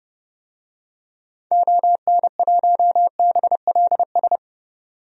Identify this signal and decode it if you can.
Morse code, heard as a keyed tone, about 30 wpm, 700 Hz. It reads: ON1BLS
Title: DXCC-100 Call Signs - 6 Characters - 30wpm